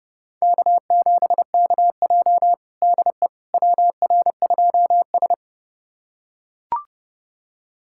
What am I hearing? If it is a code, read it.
Morse code, heard as a keyed tone, about 32 wpm, 700 Hz. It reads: K7KJ DE WR2S
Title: USA Callsign DE Callsign - 4 Characters - 30wpm